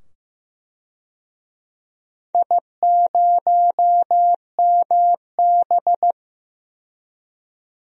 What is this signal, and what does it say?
Morse code, heard as a keyed tone, about 15 wpm, 700 Hz. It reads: I0MB
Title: DXCC-100 Call Signs - 4 Characters - 15wpm